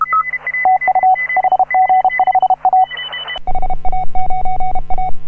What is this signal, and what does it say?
Morse code, heard as a keyed tone, about 33 wpm, 750 Hz. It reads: TU HG5A HA9A
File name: AUDIO2018 / HA9A CW